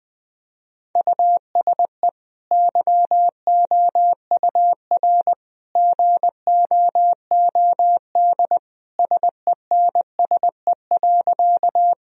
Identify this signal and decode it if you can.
Morse code, heard as a keyed tone, about 20 wpm, 700 Hz. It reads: USE YOUR GOOD SENSE.
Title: Sentences from Top 1000 Words - Review Part 4 - 20wpm